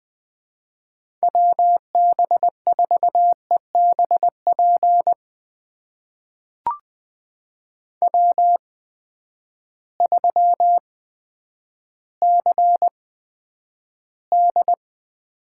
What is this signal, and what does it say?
Morse code, heard as a keyed tone, about 20 wpm, spaced as for 10 wpm, 700 Hz. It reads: WB4EBP W3CD
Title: USA Call Signs - 8x Character Spacing - 20wpm